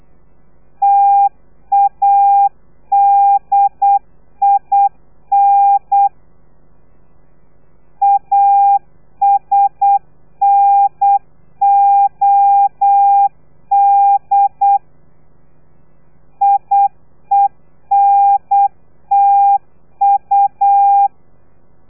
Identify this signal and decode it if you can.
Morse code, heard as a keyed tone, about 8 wpm, 800 Hz. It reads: TADIN ASNOD IENTU
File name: telegrafia / i2tfj / Lez3 35